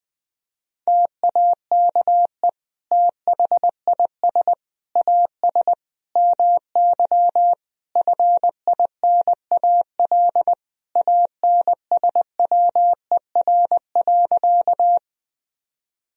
Morse code, heard as a keyed tone, about 20 wpm, 700 Hz. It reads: TAKE THIS AS MY FINAL ANSWER.
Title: Sentences from Top 1000 Words - Review Part 2 - 20wpm